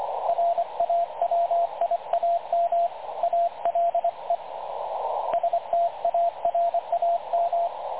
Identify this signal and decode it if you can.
Morse code, heard as a keyed tone, about 24 wpm, 680 Hz, with radio noise. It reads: RAWIAM ALE STARAM
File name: dlazaawansowanych / nagraniazpasma / websdr recording 3567.0kHz